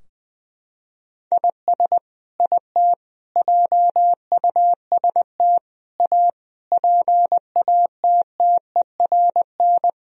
Morse code, heard as a keyed tone, about 20 wpm, 700 Hz. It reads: IS IT JUST A PATTERN
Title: Sentences from Top 1000 Words - Review Part 2 - 20wpm